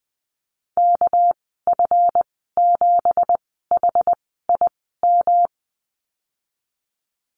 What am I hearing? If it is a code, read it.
Morse code, heard as a keyed tone, about 20 wpm, spaced as for 14 wpm, 700 Hz. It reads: KF7HIM
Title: 200 Lesson 32g - Letter k - Call Signs 20wpm